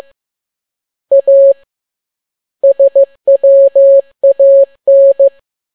Morse code, heard as a keyed tone, about 15 wpm, 550 Hz. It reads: A SWAN
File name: n6tt p1 15C13